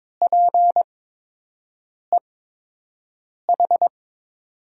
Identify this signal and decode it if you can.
Morse code, heard as a keyed tone, about 22 wpm, spaced as for 6 wpm, 700 Hz. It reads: PEH
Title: USA Call Signs - 5 Characters - 8x Character Spacing - 22wpm